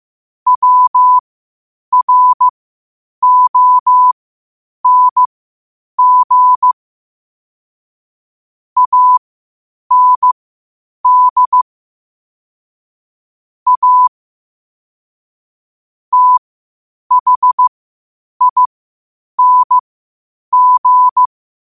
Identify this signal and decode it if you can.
Morse code, heard as a keyed tone, about 15 wpm, spaced as for 8 wpm, 1000 Hz. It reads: WRONG AND A THING
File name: qotdfiles / 8wpm